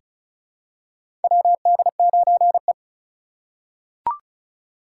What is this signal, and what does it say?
Morse code, heard as a keyed tone, about 36 wpm, 700 Hz. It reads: WD9E
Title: USA Call Signs - 4 Characters - 4x Character Spacing - 35wpm